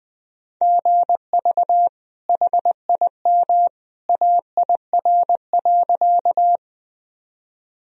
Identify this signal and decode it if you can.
Morse code, heard as a keyed tone, about 19 wpm, 700 Hz. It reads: GV HIM AIR.
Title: Sentences from Top 200 Words and Abbreviations 20wpm